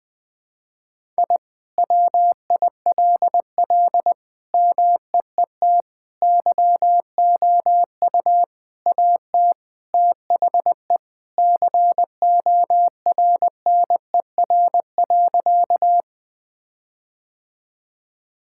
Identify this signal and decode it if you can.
Morse code, heard as a keyed tone, about 20 wpm, 700 Hz. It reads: I WILL MEET YOU AT THE CORNER.
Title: All-Age Appropriate Jokes - 20wpm